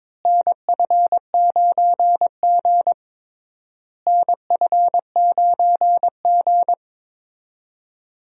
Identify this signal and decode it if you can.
Morse code, heard as a keyed tone, about 21 wpm, 700 Hz. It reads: NF9G NF9G
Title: USA Call Signs - 4 Characters - Repeated 3x - 22wpm